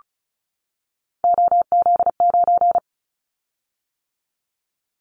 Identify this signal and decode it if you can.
Morse code, heard as a keyed tone, about 34 wpm, 700 Hz. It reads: OZ9
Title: Callsign Prefixes for Top 100 DXCC - 35wpm